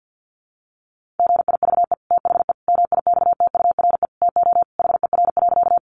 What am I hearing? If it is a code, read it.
Morse code, heard as a keyed tone, about 50 wpm, 700 Hz. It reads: GIVE THE MIXTURE TO HER.
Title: Sentences from Top 1500 Words 50wpm